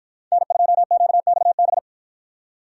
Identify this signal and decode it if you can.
Morse code, heard as a keyed tone, about 50 wpm, 700 Hz. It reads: N2QXB